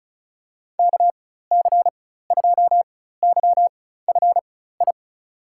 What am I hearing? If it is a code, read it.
Morse code, heard as a keyed tone, about 36 wpm, spaced as for 16 wpm, 700 Hz. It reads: KC2YFI
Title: USA Call Signs - 6 Characters - 4x Character Spacing - 35wpm